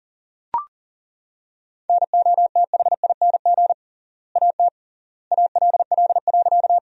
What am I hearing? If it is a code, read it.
Morse code, heard as a keyed tone, about 40 wpm, 700 Hz. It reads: NOTHING AT ALL.
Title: 3x Word-Spacing - Sentences from Top 500 Words - 40wpm